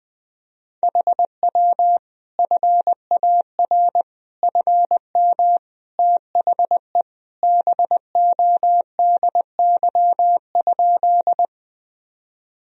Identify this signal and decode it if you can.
Morse code, heard as a keyed tone, about 21 wpm, 700 Hz. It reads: HW FAR FM THE BODY?
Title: Sentences from Top 400 Words and Abbreviations 20wpm